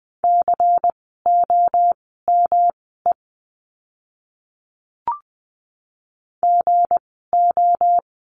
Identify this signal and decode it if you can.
Morse code, heard as a keyed tone, about 20 wpm, spaced as for 14 wpm, 700 Hz. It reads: COME GO
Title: 237 Lesson 38f - Letter q - Words 20wpm